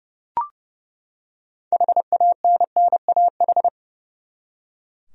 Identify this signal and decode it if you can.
Morse code, heard as a keyed tone, about 31 wpm, 700 Hz. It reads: HANNAH
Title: USA Common Names - 30wpm